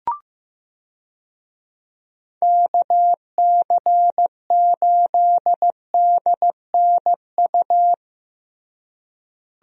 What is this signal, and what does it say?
Morse code, heard as a keyed tone, about 15 wpm, 700 Hz. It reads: KC8DNU
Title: USA Call Signs - 6 Characters - Repeated 3x - 15wpm